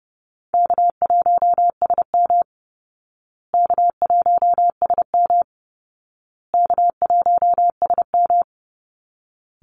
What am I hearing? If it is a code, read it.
Morse code, heard as a keyed tone, about 30 wpm, 700 Hz. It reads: K1SM K1SM K1SM
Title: Easy Call Signs Repeated 3x - 30wpm